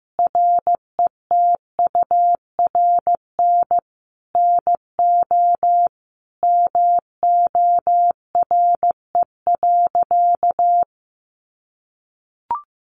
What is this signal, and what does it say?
Morse code, heard as a keyed tone, about 15 wpm, 700 Hz. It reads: RETURN NO MORE.
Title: Sentences from Top 800 Words 15wpm